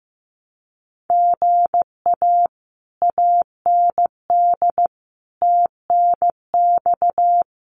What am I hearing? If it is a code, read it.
Morse code, heard as a keyed tone, about 15 wpm, 700 Hz. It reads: GA AND TNX
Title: Common QSO Phrases Encore 15wpm